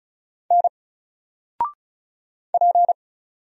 Morse code, heard as a keyed tone, about 34 wpm, 700 Hz. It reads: N P
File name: Single-Letter-Number-35wpm